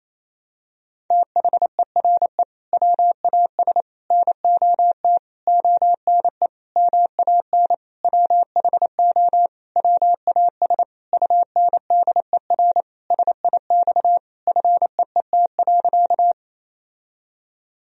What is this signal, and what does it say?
Morse code, heard as a keyed tone, about 27 wpm, 700 Hz. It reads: THERE WAS NOT ONE MAN WHO WAS UNDER SIX FEET.
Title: Sentences from Top 1000 Words - Review Part 2 - 28wpm